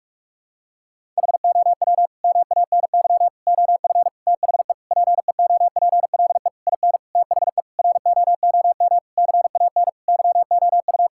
Morse code, heard as a keyed tone, about 46 wpm, 700 Hz. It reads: HOW MANY OF THE PEOPLE IN THE ROOM CAN YOU
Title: Sentences from Top 1000 Words - Review Part 6 - 45wpm